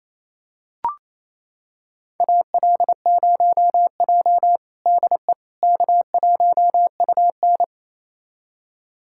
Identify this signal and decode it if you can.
Morse code, heard as a keyed tone, about 28 wpm, 700 Hz. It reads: AL0J DE K1UN
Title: USA Callsign DE Callsign - 4 Characters - 28wpm